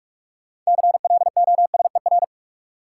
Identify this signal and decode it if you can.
Morse code, heard as a keyed tone, about 46 wpm, 700 Hz. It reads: CLOSER
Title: Sets of 5 Words Encore - New Word Drill - 45wpm